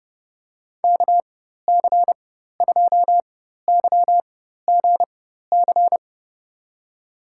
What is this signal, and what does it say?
Morse code, heard as a keyed tone, about 30 wpm, spaced as for 14 wpm, 700 Hz. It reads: KC2YGC
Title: USA Call Signs - 6 Characters - 4x Character Spacing - 30wpm